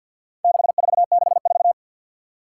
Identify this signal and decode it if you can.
Morse code, heard as a keyed tone, about 51 wpm, 700 Hz. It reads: 6464
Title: Four Challenging Letter-Number Combinations 50wpm